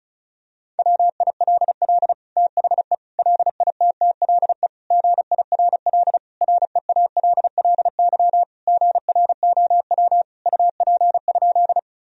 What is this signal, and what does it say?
Morse code, heard as a keyed tone, about 36 wpm, 700 Hz. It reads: WILL THE LITTLE GIRL REALLY GROW UP?
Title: Sentences from Top 1000 Words - Review Part 1 - 35wpm